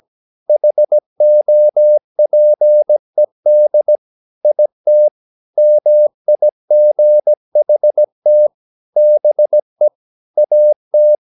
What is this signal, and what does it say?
Morse code, heard as a keyed tone, about 17 wpm, 600 Hz. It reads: HOPED IT MIGHT BE AT
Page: Romanen Never af Ken Follet på engelsk.